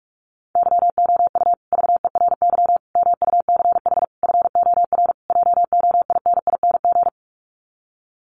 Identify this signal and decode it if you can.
Morse code, heard as a keyed tone, about 44 wpm, 700 Hz. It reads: YOU VERY MUCH FOR JOINING
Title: Sets of 5 Words Encore 45wpm